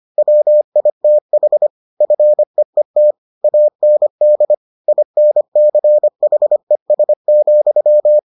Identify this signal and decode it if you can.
Morse code, heard as a keyed tone, about 25 wpm, 600 Hz. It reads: WITH FEET AND INCHES,